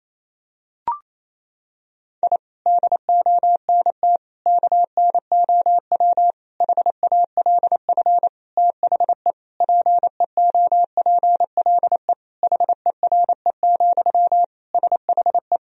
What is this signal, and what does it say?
Morse code, heard as a keyed tone, about 27 wpm, 700 Hz. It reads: I DONT KNOW HALF THE PEOPLE HERE, SHE
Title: Sentences from Top 1000 Words - Review Part 1 - 28wpm